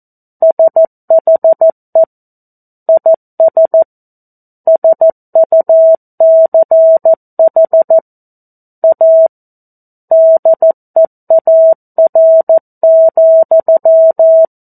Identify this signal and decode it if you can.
Morse code, heard as a keyed tone, about 14 wpm, 650 Hz. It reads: SHE IS SUCH A DEAR,
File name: 02-Chapter-02 14 WPM